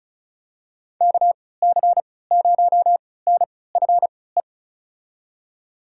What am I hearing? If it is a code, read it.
Morse code, heard as a keyed tone, about 34 wpm, spaced as for 20 wpm, 700 Hz. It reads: KC0NFE
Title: USA Call Signs - 6 Characters - 3x Character Spacing - 35wpm